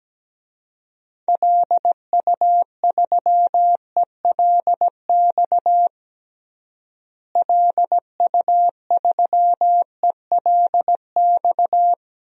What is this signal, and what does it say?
Morse code, heard as a keyed tone, about 17 wpm, 700 Hz. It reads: LU3ELX LU3ELX
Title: DXCC-100 Call Signs - 6 Characters - Repeated 3x - 17wpm